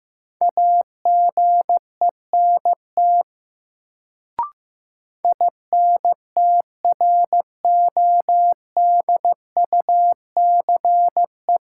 Morse code, heard as a keyed tone, about 15 wpm, 700 Hz. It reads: AGENT INTRODUCE
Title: Sentences from Top 1600 Words - New Word Drill - 15wpm